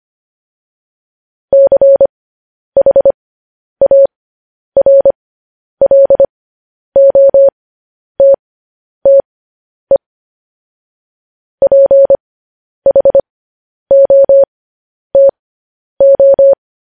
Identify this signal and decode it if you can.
Morse code, heard as a keyed tone, about 25 wpm, spaced as for 10 wpm, 550 Hz. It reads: CHARLOTTE PHOTO
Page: news-headlines-in-morse-code-at-10-wpm